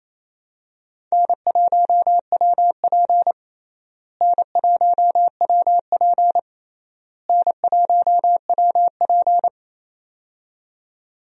Morse code, heard as a keyed tone, about 28 wpm, 700 Hz. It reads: N1WP N1WP N1WP
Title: USA Call Signs - 4 Characters - Repeated 3x - 28wpm